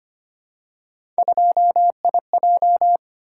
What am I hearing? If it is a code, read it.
Morse code, heard as a keyed tone, about 25 wpm, 700 Hz. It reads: 2IJ
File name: Three Letter-Number Combinations - 25wpm